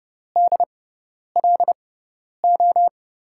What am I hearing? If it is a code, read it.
Morse code, heard as a keyed tone, about 32 wpm, spaced as for 10 wpm, 700 Hz. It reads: DLO
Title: USA Call Signs - 6x Character Spacing - 30wpm